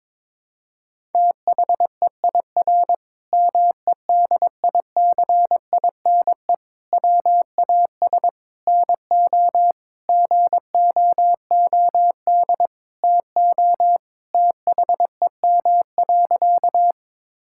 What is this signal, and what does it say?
Morse code, heard as a keyed tone, about 22 wpm, 700 Hz. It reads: THEIR MEDICINE WAS NO GOOD TO THEM.
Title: Sentences from Top 1000 Words - Review Part 6 - 22wpm